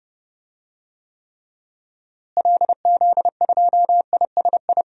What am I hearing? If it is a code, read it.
Morse code, heard as a keyed tone, about 30 wpm, 700 Hz. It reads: LZ2ISI
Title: DXCC-100 Call Signs - 6 Characters - Repeated 3x - 30wpm